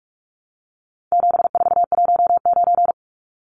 Morse code, heard as a keyed tone, about 44 wpm, 700 Hz. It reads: 7419
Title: Numbers - 4-Digit 45wpm